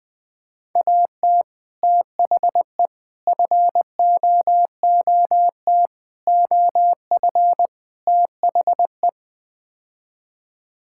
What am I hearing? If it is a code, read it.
Morse code, heard as a keyed tone, about 21 wpm, 700 Hz. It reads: AT THE FOOT OF THE
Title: Build A Phrase - 5-Word Phrases - 20wpm